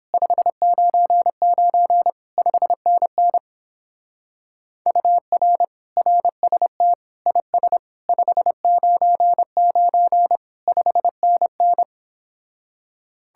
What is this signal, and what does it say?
Morse code, heard as a keyed tone, about 31 wpm, 700 Hz. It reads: 599 5NN UR RST IS 599 5NN